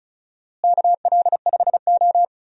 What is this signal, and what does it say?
Morse code, heard as a keyed tone, about 36 wpm, 700 Hz. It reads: KL5O
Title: USA Call Signs - 4 Characters - Repeated 3x - 35wpm